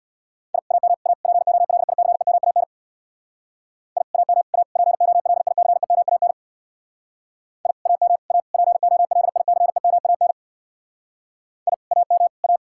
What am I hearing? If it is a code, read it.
Morse code, heard as a keyed tone, about 62 wpm, 700 Hz. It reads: I AM A POLICEMAN I AM A POLICEMAN I AM A POLICEMAN I AM A
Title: Speed-Racing Common QSO Phrases - 50wpm